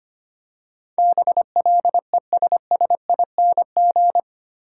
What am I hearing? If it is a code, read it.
Morse code, heard as a keyed tone, about 25 wpm, 700 Hz. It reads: BLESSING
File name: 8 Letter Words - First 3 Letters Given - 25wpm